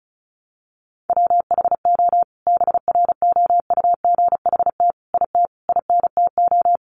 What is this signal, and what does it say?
Morse code, heard as a keyed tone, about 34 wpm, 700 Hz. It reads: WHO BROUGHT IT INTO
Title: Sentences from Top 600 Words 35wpm